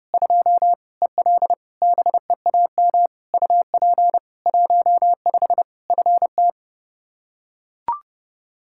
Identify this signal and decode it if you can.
Morse code, heard as a keyed tone, about 30 wpm, 700 Hz. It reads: 2 EL BEAM UP 15 FT
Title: K1USN SST QSO Practice 30wpm